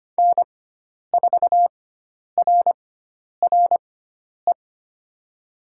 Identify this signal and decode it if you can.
Morse code, heard as a keyed tone, about 25 wpm, spaced as for 10 wpm, 700 Hz. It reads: N4RRE
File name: Lesson 16g - Letter c - Call Signs 25wpm